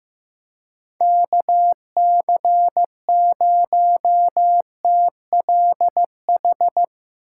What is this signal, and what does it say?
Morse code, heard as a keyed tone, about 15 wpm, 700 Hz. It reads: KC0TLH
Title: USA Call Signs - 6 Characters - 4x Character Spacing - 15wpm